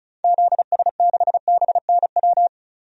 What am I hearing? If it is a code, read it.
Morse code, heard as a keyed tone, about 34 wpm, 700 Hz. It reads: ZS6BNW
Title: DXCC-100 Call Signs - 6 Characters - 35wpm